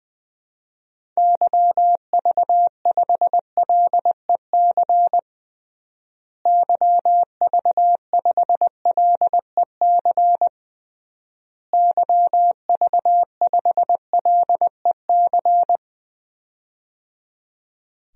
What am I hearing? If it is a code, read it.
Morse code, heard as a keyed tone, about 21 wpm, 700 Hz. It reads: YV5LEC YV5LEC YV5LEC
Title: DXCC-100 Call Signs - 6 Characters - Repeated 3x - 20wpm